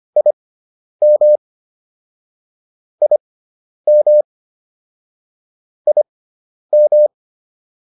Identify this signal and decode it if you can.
Morse code, heard as a keyed tone, about 25 wpm, spaced as for 10 wpm, 600 Hz. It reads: IM IM IM